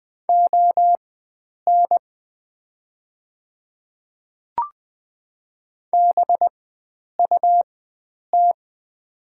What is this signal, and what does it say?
Morse code, heard as a keyed tone, about 20 wpm, spaced as for 9 wpm, 700 Hz. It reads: ON BUT
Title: 4x Character Spacing - Top 100 Words - 20wpm